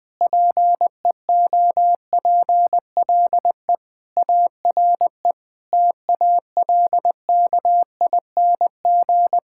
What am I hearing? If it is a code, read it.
Morse code, heard as a keyed tone, about 20 wpm, 700 Hz. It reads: PEOPLE ARE TALKING
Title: Sentences with 5-Word Phrases - Spoken Beginning - 20wpm